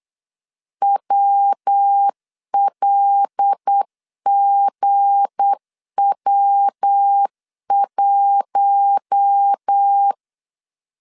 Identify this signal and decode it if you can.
Morse code, heard as a keyed tone, about 8 wpm, 800 Hz. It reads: WLGW1
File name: LEZIONI / 7 GF9Z / 7 GF9Z 8wpm 3213